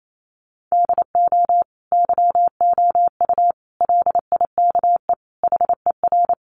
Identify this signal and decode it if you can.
Morse code, heard as a keyed tone, about 28 wpm, 700 Hz. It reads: DO YOU LIKE HER
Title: Sentences from Top 1100 Words 28wpm